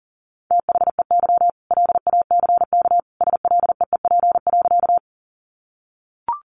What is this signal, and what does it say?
Morse code, heard as a keyed tone, about 40 wpm, 700 Hz. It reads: THEY LACK SLEEP.
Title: Sentences from Top 1100 Words 40wpm